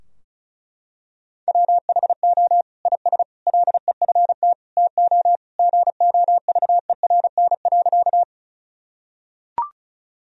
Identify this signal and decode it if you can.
Morse code, heard as a keyed tone, about 36 wpm, 700 Hz. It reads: WHO IS LEFT TO GOVERN.
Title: Sentences from Top 1000 Words - Review Part 2 - 35wpm